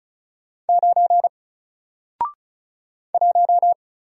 Morse code, heard as a keyed tone, about 35 wpm, 700 Hz. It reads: 9 1
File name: Single-Letter-Number-35wpm